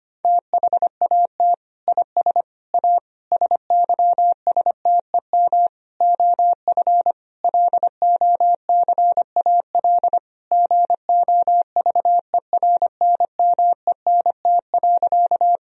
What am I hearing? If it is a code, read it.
Morse code, heard as a keyed tone, about 25 wpm, 700 Hz. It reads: THAT IS A SYSTEM OF LOCAL GOVERNMENT.
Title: Sentences from Top 1000 Words - Review Part 3 - 25wpm